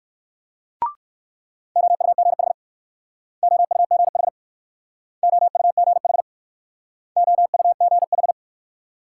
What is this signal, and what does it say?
Morse code, heard as a keyed tone, about 59 wpm, 690 Hz. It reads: OUGH OUGH OUGH OUGH